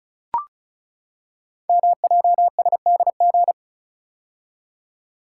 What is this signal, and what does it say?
Morse code, heard as a keyed tone, about 36 wpm, 700 Hz. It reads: MJSDG
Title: Code Groups - 5 Letters - 35wpm